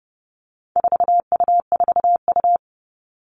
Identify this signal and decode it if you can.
Morse code, heard as a keyed tone, about 30 wpm, 700 Hz. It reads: 4U4U
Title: All Cut Numbers - 4-Digit 30wpm